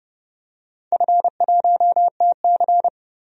Morse code, heard as a keyed tone, about 32 wpm, 700 Hz. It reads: F1TC
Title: DXCC-100 Call Signs - 4 Characters - 30wpm